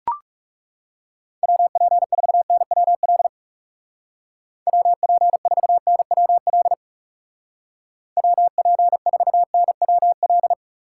Morse code, heard as a keyed tone, about 36 wpm, 700 Hz. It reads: WP4NWL WP4NWL WP4NWL
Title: Speed Racing - USA 2x3 Call Signs 30wpm